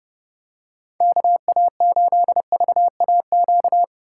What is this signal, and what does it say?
Morse code, heard as a keyed tone, about 29 wpm, 700 Hz. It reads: KA8VAQ